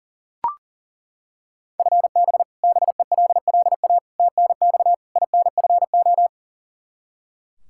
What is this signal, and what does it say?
Morse code, heard as a keyed tone, about 40 wpm, 700 Hz. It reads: FB BELLA TNX INFO
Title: Practice QSO - Exchange 3 - More Info - 40wpm